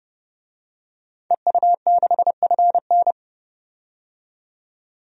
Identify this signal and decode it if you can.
Morse code, heard as a keyed tone, about 32 wpm, 700 Hz. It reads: EU6FN
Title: DXCC-100 Call Signs - 5 Characters - 30wpm